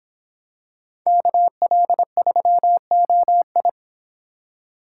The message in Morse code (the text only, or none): KL3OI